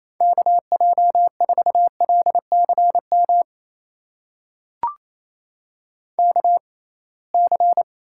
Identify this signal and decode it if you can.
Morse code, heard as a keyed tone, about 28 wpm, spaced as for 21 wpm, 700 Hz. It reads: KJ4LCM KC
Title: USA Call Signs - 6 Characters - 6x Character Spacing - 28wpm